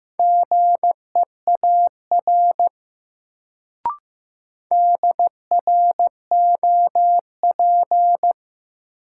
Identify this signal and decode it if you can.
Morse code, heard as a keyed tone, about 15 wpm, 700 Hz. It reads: GEAR DROP
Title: Sentences from Top 1400 Words - New Word Drill - 15wpm